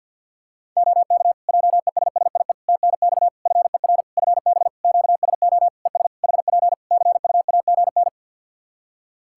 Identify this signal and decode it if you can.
Morse code, heard as a keyed tone, about 51 wpm, 700 Hz. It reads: OK JESSIE TNX FER FB QSO ES HP CUAGN
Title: Practice QSO - Exchange 4 - Ending - 50wpm